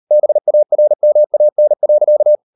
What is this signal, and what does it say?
Morse code, heard as a keyed tone, about 38 wpm, 600 Hz. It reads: BARMAN.